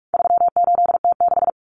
Morse code, heard as a keyed tone, about 44 wpm, 700 Hz. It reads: 38T6
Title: Common Cut Numbers - 4-Digit 45wpm